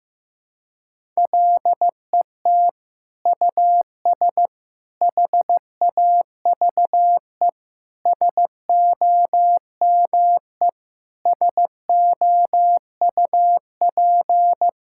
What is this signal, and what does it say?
Morse code, heard as a keyed tone, about 15 wpm, 700 Hz. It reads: LET US HAVE SOME SOUP
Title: Sentences from Top 1000 Words - Review Part 6 - 15wpm